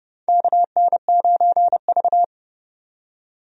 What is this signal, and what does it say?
Morse code, heard as a keyed tone, about 31 wpm, 700 Hz. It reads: KN9V
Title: USA Call Signs - 4 Characters - 30wpm